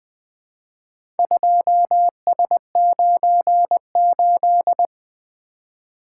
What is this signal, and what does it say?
Morse code, heard as a keyed tone, about 20 wpm, 700 Hz. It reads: 2S98
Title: Four Letter-Number Combinations - 20wpm